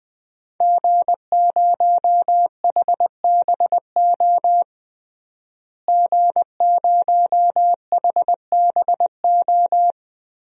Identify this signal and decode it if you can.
Morse code, heard as a keyed tone, about 21 wpm, 700 Hz. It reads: G0HBO G0HBO
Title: DXCC-100 Call Signs - 5 Characters - Repeated 3x - 20wpm